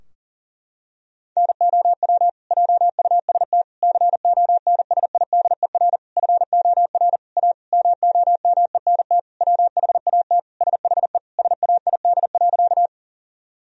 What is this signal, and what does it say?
Morse code, heard as a keyed tone, about 40 wpm, 700 Hz. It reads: NOW JUST CONSIDER FOR A MOMENT WHAT SHE SAID.
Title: Sentences from Top 1000 Words - Review Part 4 - 40wpm